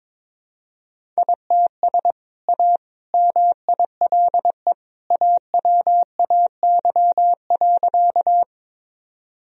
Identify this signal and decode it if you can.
Morse code, heard as a keyed tone, about 22 wpm, 700 Hz. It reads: ITS A MILE AWAY.
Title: Sentences from Top 1000 Words - Review Part 1 - 22wpm